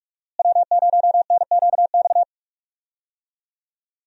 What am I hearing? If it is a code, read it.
Morse code, heard as a keyed tone, about 46 wpm, 700 Hz. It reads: W0NQX
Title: USA Call Signs - 5 Characters - Repeated 2x - 45wpm